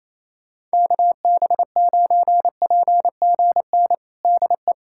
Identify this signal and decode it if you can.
Morse code, heard as a keyed tone, about 28 wpm, 700 Hz. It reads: KB9PGN DE
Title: USA Callsign DE Callsign - 6 Characters - 28wpm